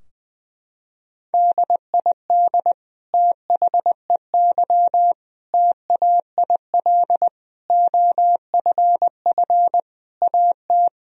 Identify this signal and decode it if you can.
Morse code, heard as a keyed tone, about 20 wpm, 700 Hz. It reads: DID THEY TAIL OFF AT
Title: Sentences from Top 1000 Words - Review Part 3 - 20wpm